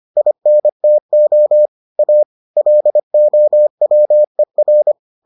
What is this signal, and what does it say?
Morse code, heard as a keyed tone, about 25 wpm, 600 Hz. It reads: INTO A LOWER